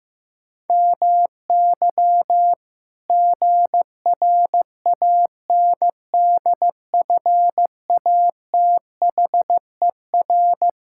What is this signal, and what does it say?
Morse code, heard as a keyed tone, about 16 wpm, 700 Hz. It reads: MY GRANDFATHER
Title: Pangram Challenge v2 - 15wpm